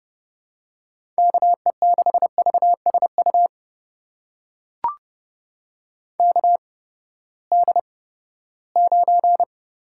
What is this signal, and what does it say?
Morse code, heard as a keyed tone, about 30 wpm, spaced as for 17 wpm, 700 Hz. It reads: KE6VSU KD9
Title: USA Call Signs - 6 Characters - 8x Character Spacing - 30wpm